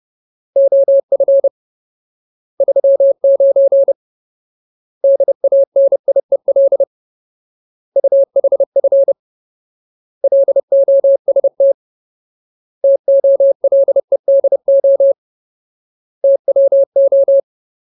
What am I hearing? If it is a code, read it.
Morse code, heard as a keyed tone, about 30 wpm, 550 Hz. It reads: OF 39 DANIEL UHF LOST TOLEDO TWO
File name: Ultimate Word List / 30WPM Ultimate